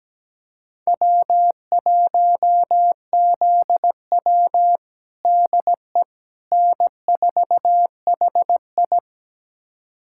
Morse code, heard as a keyed tone, about 17 wpm, 700 Hz. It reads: W1ZW DE N4HI
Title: USA Callsign DE Callsign - 4 Characters - 17wpm